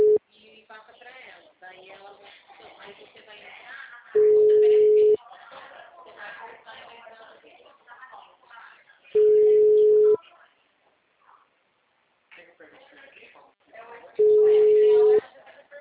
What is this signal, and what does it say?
Ringback tone (Europe, 425 Hz), with speech.